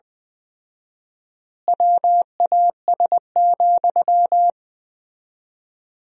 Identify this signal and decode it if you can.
Morse code, heard as a keyed tone, about 20 wpm, 700 Hz. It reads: WAS,
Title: 4x Word-Spacing - Sentences from Top 400 Words - 20wpm